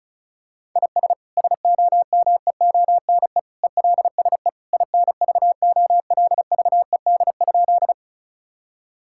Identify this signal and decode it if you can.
Morse code, heard as a keyed tone, about 34 wpm, 700 Hz. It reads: IS SOMEONE ELSE INVOLVED?
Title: Sentences from Top 1000 Words - Review Part 5 - 35wpm